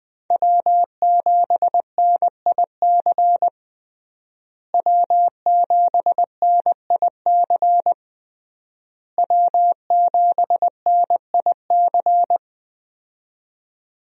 Morse code, heard as a keyed tone, about 20 wpm, 700 Hz. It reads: W7NIC W7NIC W7NIC
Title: USA Call Signs - 5 Characters - Repeated 3x - 20wpm